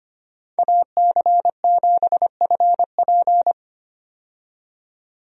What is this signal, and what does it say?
Morse code, heard as a keyed tone, about 25 wpm, 700 Hz. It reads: AC7FP
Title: USA Call Signs - 5 Characters - 25wpm